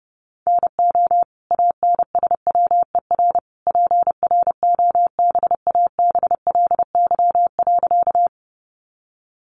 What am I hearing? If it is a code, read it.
Morse code, heard as a keyed tone, about 30 wpm, 700 Hz. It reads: NO ANSWER PROBABLY.
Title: Sentences from Top 700 Words 30wpm